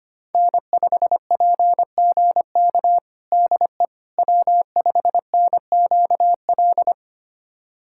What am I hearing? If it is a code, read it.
Morse code, heard as a keyed tone, about 25 wpm, 700 Hz. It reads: N5PGK DE W5NQL
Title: USA Callsign DE Callsign - 5 Characters - 25wpm